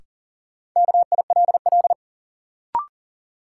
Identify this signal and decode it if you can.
Morse code, heard as a keyed tone, about 40 wpm, 700 Hz. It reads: KILL
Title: Sentences from Top 1400 Words - New Word Drill - 40wpm